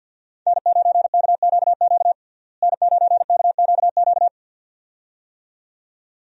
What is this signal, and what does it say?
Morse code, heard as a keyed tone, about 46 wpm, 700 Hz. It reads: N9KQQ N9KQQ
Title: USA Call Signs - 5 Characters - Repeated 2x - 50wpm